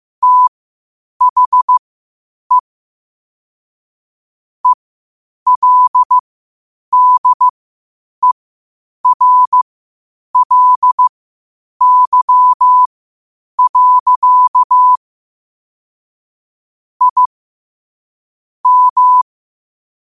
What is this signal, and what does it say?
Morse code, heard as a keyed tone, about 15 wpm, spaced as for 8 wpm, 1000 Hz. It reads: THE ELDERLY. I M
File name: qotdfiles / 8wpm